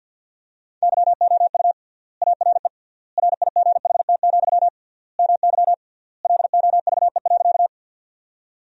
Morse code, heard as a keyed tone, about 46 wpm, 700 Hz. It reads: YOU ARE RIGHT, MY LOVE.
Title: 3x Word-Spacing - Sentences from Top 400 Words - 50wpm